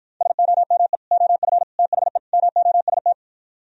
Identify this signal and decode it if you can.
Morse code, heard as a keyed tone, about 52 wpm, 700 Hz. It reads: SOME OF THE MOST